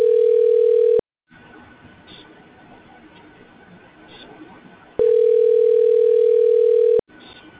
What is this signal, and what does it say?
Ringback tone (North America), 440 + 480 Hz.